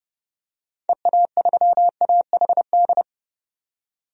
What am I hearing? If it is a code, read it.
Morse code, heard as a keyed tone, about 30 wpm, 700 Hz. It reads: EA3AHD
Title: DXCC-100 Call Signs - 6 Characters - 30wpm